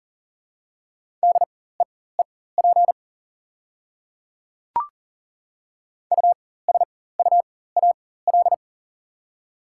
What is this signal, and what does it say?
Morse code, heard as a keyed tone, about 40 wpm, spaced as for 19 wpm, 690 Hz. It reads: DEEP USUAL
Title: 4x Character Spacing - Top 500 Words - 40wpm